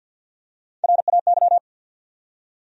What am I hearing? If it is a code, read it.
Morse code, heard as a keyed tone, about 50 wpm, 700 Hz. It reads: RAY
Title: USA Common Names - Max 4 Letters - 50wpm